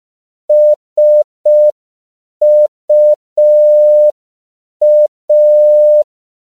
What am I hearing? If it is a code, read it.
Morse code, heard as a keyed tone, about 5 wpm, 600 Hz. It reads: SUA